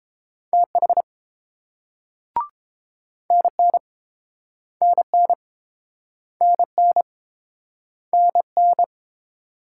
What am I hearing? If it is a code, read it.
Morse code, heard as a keyed tone, about 32 wpm, 700 Hz. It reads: TH NN NN NN NN